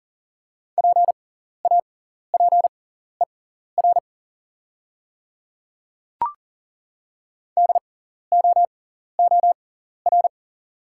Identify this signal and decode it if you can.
Morse code, heard as a keyed tone, about 40 wpm, spaced as for 14 wpm, 690 Hz. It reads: PAPER DOOR
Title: 6x Character Spacing - Top 300 Words - 40wpm